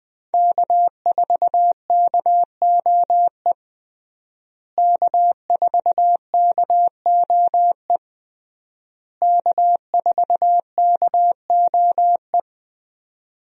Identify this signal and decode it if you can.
Morse code, heard as a keyed tone, about 20 wpm, 700 Hz. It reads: K4KOE K4KOE K4KOE
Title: USA Call Signs - 5 Characters - Repeated 3x - 20wpm